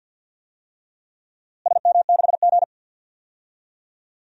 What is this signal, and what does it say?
Morse code, heard as a keyed tone, about 51 wpm, 700 Hz. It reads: SM6G
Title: DXCC-100 Call Signs - 4 Characters - Repeated 3x - 50wpm